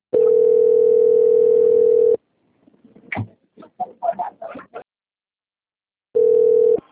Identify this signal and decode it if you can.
Ringback tone (North America), 440 + 480 Hz.